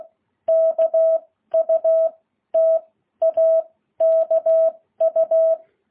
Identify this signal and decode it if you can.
Morse code, heard as a keyed tone, about 16 wpm, spaced as for 13 wpm, 660 Hz. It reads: KU TAKU
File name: jh8cbh